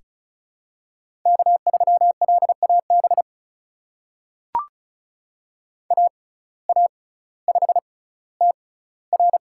Morse code, heard as a keyed tone, about 36 wpm, spaced as for 18 wpm, 700 Hz. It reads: K3LAB AA5TR
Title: USA Call Signs - 6x Character Spacing - 35wpm